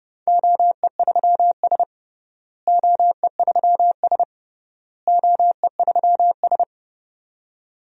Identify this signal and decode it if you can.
Morse code, heard as a keyed tone, about 30 wpm, 700 Hz. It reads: OE3S OE3S OE3S
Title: DXCC-100 Call Signs - 4 Characters - Repeated 3x - 30wpm